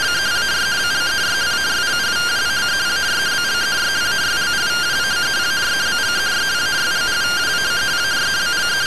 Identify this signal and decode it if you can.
Two-tone teleprinter signal, about 170 Hz shift.